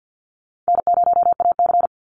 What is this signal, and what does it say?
Morse code, heard as a keyed tone, about 50 wpm, 700 Hz. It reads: N0AC
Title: Easy Call Signs - 50wpm